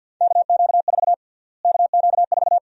Morse code, heard as a keyed tone, about 51 wpm, 700 Hz. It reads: KQ4 KQ4
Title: USA Call Sign Prefixes - Repeated 2x - 50wpm